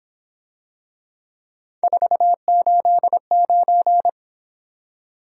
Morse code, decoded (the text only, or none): 489